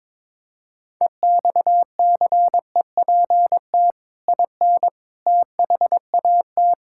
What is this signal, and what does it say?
Morse code, heard as a keyed tone, about 22 wpm, 700 Hz. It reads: EXCEPT IN THAT
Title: Sentences from Top 1600 Words 22wpm